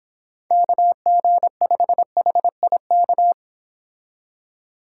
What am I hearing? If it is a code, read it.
Morse code, heard as a keyed tone, about 26 wpm, 700 Hz. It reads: KG5HIK